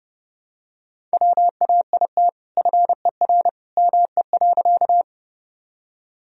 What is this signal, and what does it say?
Morse code, heard as a keyed tone, about 32 wpm, 700 Hz. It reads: WAIT FER ME.
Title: Sentences from Top 500 Words and Abbreviations 30wpm